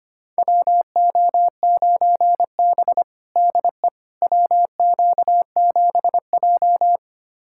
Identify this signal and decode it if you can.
Morse code, heard as a keyed tone, about 25 wpm, 700 Hz. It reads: WO9B DE WQ7J
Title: USA Callsign DE Callsign - 4 Characters - 25wpm